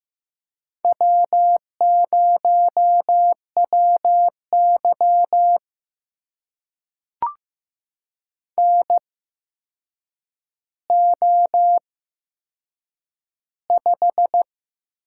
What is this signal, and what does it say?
Morse code, heard as a keyed tone, about 15 wpm, spaced as for 7 wpm, 700 Hz. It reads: W0WY NO5
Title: USA Call Signs - 4 Characters - 8x Character Spacing - 15wpm